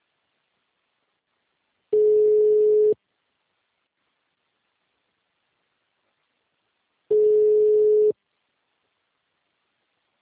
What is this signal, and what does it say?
Ringback tone (Europe, 425 Hz).
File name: out- -2003- - - .22504